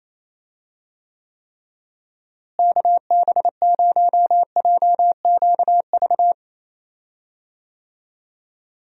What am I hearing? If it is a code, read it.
Morse code, heard as a keyed tone, about 28 wpm, 700 Hz. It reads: KB0JQV
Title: USA Call Signs - 6 Characters - Repeated 3x - 28wpm